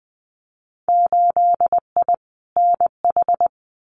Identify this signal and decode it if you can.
Morse code, heard as a keyed tone, about 20 wpm, 700 Hz. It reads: 8I NH
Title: Winter Field Day Exchanges Encore 20wpm